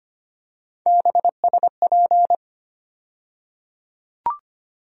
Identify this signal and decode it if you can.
Morse code, heard as a keyed tone, about 25 wpm, 700 Hz. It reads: BSP
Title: USA Call Sign Postfixes - 3 Letters 25wpm